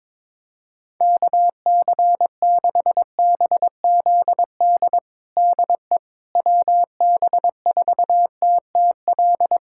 Morse code, heard as a keyed tone, about 21 wpm, 700 Hz. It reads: KC6BZD DE WB4TTL
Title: USA Callsign DE Callsign - 6 Characters - 22wpm